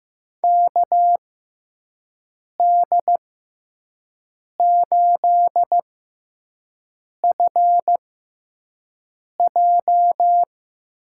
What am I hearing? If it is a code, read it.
Morse code, heard as a keyed tone, about 15 wpm, spaced as for 5 wpm, 700 Hz. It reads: KD8FJ
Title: USA Call Signs - 5 Characters - 6x Character Spacing - 15wpm